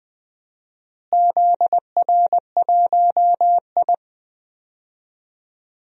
Morse code, heard as a keyed tone, about 20 wpm, 700 Hz. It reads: ZR1I
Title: DXCC-100 Call Signs - 4 Characters - Repeated 3x - 20wpm